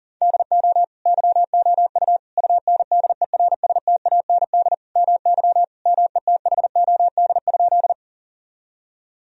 Morse code, heard as a keyed tone, about 40 wpm, 700 Hz. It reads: DO YOU UNDERSTAND MY METHOD?
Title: Sentences from Top 1000 Words - Review Part 3 - 40wpm